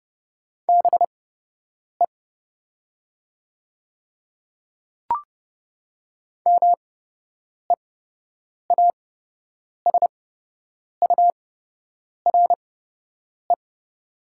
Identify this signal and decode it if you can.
Morse code, heard as a keyed tone, about 30 wpm, spaced as for 8 wpm, 690 Hz. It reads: BE MEASURE
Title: 8x Character Spacing - Top 500 Words - 30wpm